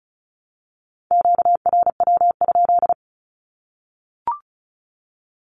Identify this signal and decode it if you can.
Morse code, heard as a keyed tone, about 34 wpm, 700 Hz. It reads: QRW?
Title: Q-Codes 35wpm